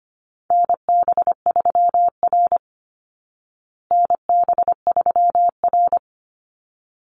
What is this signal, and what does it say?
Morse code, heard as a keyed tone, about 25 wpm, 700 Hz. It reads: NB3R NB3R
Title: Call Signs Repeated 3x - 25wpm